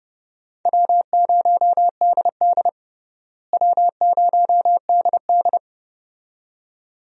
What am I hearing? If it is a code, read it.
Morse code, heard as a keyed tone, about 30 wpm, 700 Hz. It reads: W0DD W0DD
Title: USA Call Signs - 4 Characters - Repeated 2x - 30wpm